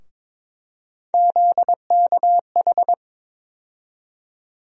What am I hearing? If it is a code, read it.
Morse code, heard as a keyed tone, about 22 wpm, 700 Hz. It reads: ZKH
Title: USA Call Sign Postfixes - 3 Letters 22wpm